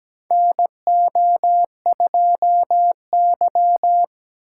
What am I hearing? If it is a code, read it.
Morse code, heard as a keyed tone, about 17 wpm, 700 Hz. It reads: NO2Y
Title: USA Call Signs - 4 Characters - 4x Character Spacing - 17wpm